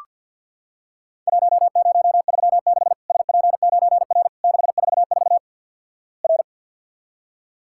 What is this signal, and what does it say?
Morse code, heard as a keyed tone, about 47 wpm, 700 Hz. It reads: 1036 SP9R 644 R
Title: Worked All Europe DX Contest - QTC Exchange with Repeats - 50wpm